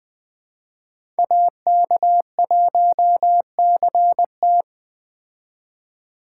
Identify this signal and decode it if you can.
Morse code, heard as a keyed tone, about 19 wpm, 700 Hz. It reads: AK1CT